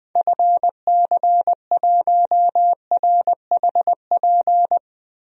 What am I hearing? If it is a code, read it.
Morse code, heard as a keyed tone, about 20 wpm, 700 Hz. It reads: FC1RHP
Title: DXCC-100 Call Signs - 6 Characters - Repeated 3x - 20wpm